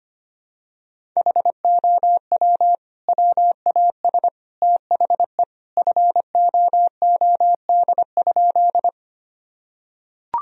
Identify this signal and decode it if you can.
Morse code, heard as a keyed tone, about 25 wpm, 700 Hz. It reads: HOW WAS THE FOOD?
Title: Sentences from Top 1000 Words - Review Part 1 - 25wpm